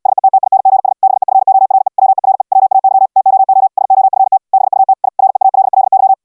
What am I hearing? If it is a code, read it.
Morse code, heard as a keyed tone, about 69 wpm, 770 Hz. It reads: SETTINGS BEFORE WERE WAY TOO SOFT BUT I REALLY